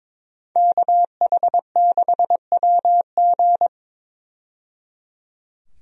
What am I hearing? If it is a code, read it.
Morse code, heard as a keyed tone, about 22 wpm, 700 Hz. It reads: KH6WG
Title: DXCC-100 Call Signs - 5 Characters - 22wpm